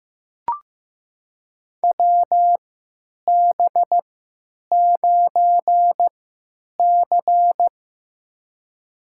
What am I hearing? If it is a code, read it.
Morse code, heard as a keyed tone, about 15 wpm, spaced as for 9 wpm, 700 Hz. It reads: WB9C
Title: USA Call Signs - 4 Characters - 3x Character Spacing - 15wpm